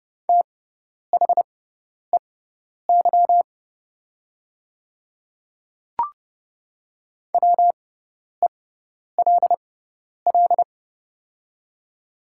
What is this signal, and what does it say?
Morse code, heard as a keyed tone, about 30 wpm, spaced as for 10 wpm, 700 Hz. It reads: THEY WELL
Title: 6x Character Spacing - Top 200 Words - 30wpm